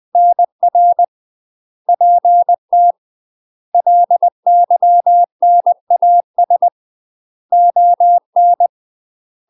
Morse code, heard as a keyed tone, about 20 wpm, 700 Hz. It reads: NR PT LYNAS ON
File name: 20WPM